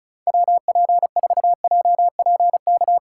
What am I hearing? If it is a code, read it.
Morse code, heard as a keyed tone, about 42 wpm, 700 Hz. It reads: WP4JPK
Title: Speed Racing - USA 2x3 Call Signs 35wpm